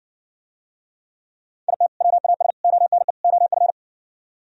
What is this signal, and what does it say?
Morse code, heard as a keyed tone, about 69 wpm, 700 Hz. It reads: IT WAS ONE OF